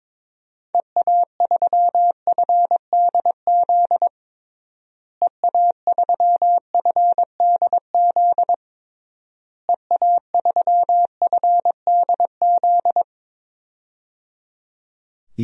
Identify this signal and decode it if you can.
Morse code, heard as a keyed tone, about 22 wpm, 700 Hz. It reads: EA3FDZ EA3FDZ EA3FDZ
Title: DXCC-100 Call Signs - 6 Characters - Repeated 3x - 22wpm